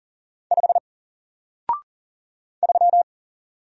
Morse code, heard as a keyed tone, about 40 wpm, 700 Hz. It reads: 5 3
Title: Single Letter-Number Alt 40wpm